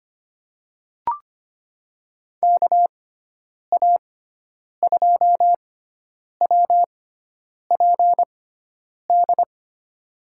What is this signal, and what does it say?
Morse code, heard as a keyed tone, about 25 wpm, spaced as for 9 wpm, 700 Hz. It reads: KA2WPD
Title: USA Call Signs - 6x Character Spacing - 25wpm